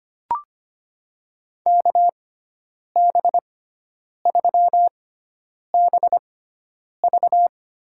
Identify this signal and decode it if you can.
Morse code, heard as a keyed tone, about 25 wpm, spaced as for 9 wpm, 700 Hz. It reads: KB3BV
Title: USA Call Signs - 6x Character Spacing - 25wpm